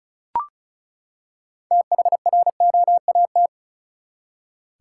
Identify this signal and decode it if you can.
Morse code, heard as a keyed tone, about 36 wpm, 700 Hz. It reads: THROAT
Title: Six Letter Words - 35wpm